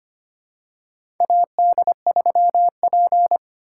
Morse code, heard as a keyed tone, about 25 wpm, 700 Hz. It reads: AD3P
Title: USA Call Signs - 2x Character Spacing - 25wpm